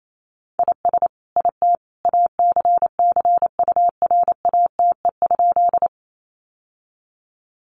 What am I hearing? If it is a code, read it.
Morse code, heard as a keyed tone, about 28 wpm, 700 Hz. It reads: IS IT ACCURATE?
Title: Sentences from Top 1200 Words 28wpm